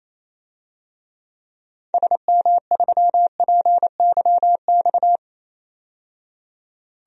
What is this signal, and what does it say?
Morse code, heard as a keyed tone, about 28 wpm, 700 Hz. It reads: SM3PYX
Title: DXCC-100 Call Signs - 6 Characters - 28wpm